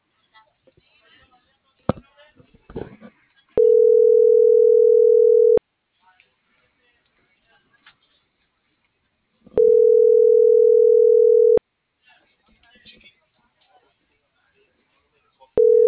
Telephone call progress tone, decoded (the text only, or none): ringback tone